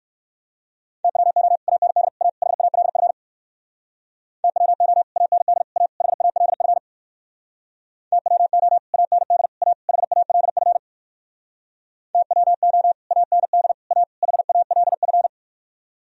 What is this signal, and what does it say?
Morse code, heard as a keyed tone, about 58 wpm, 700 Hz. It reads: TWO AND A HALF TWO AND A HALF TWO AND A HALF TWO AND A HALF